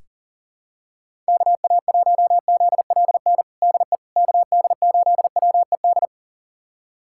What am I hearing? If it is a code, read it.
Morse code, heard as a keyed tone, about 37 wpm, 700 Hz. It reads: KA1ZLN DE KD8WED
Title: USA Callsign DE Callsign - 6 Characters - 40wpm